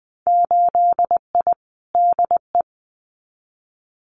Morse code, heard as a keyed tone, about 20 wpm, 700 Hz. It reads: 8I DE
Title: Winter Field Day Exchanges Encore 20wpm